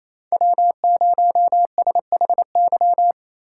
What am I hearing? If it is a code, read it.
Morse code, heard as a keyed tone, about 36 wpm, 700 Hz. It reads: W0SHY